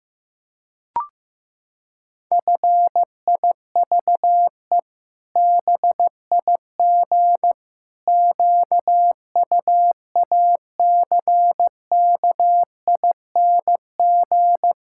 Morse code, heard as a keyed tone, about 14 wpm, 700 Hz. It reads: FIVE BIG QUACKING
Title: Pangram Challenge v2 - 15wpm